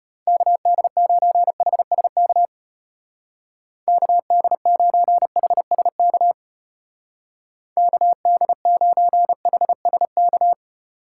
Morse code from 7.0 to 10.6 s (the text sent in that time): KD9HSK